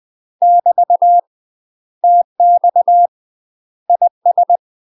Morse code, heard as a keyed tone, about 20 wpm, 700 Hz. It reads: <BT> TX IS
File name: UK MASTER QSOs TRAINING / 20WPM / UK MASTER QSOs 0041